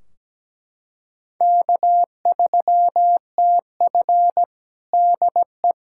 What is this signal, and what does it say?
Morse code, heard as a keyed tone, about 17 wpm, 700 Hz. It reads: K3TF DE
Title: Practice QSO - Exchange 4 - Ending - 17wpm